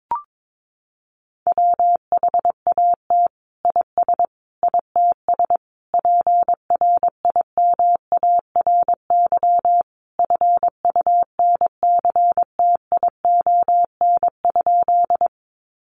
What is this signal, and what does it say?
Morse code, heard as a keyed tone, about 22 wpm, 700 Hz. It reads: WHAT IS ITS PRIMARY FUNCTION?
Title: Sentences from Top 1100 Words 22wpm